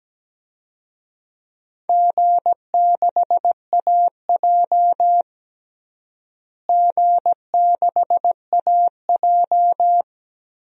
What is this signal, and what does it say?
Morse code, heard as a keyed tone, about 17 wpm, 700 Hz. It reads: G6AJ G6AJ
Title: DXCC-100 Call Signs - 4 Characters - Repeated 2x - 17wpm